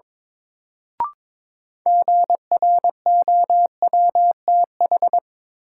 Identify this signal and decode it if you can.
Morse code, heard as a keyed tone, about 22 wpm, 700 Hz. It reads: GROWTH
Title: Sentences from Top 1000 Words - Review Part 5 - New Word Drill - 22wpm